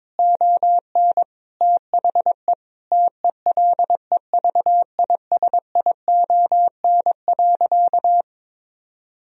Morse code, heard as a keyed tone, about 21 wpm, 700 Hz. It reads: ON THE TELEVISION.
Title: Sentences from Top 1000 Words - Review Part 4 - 22wpm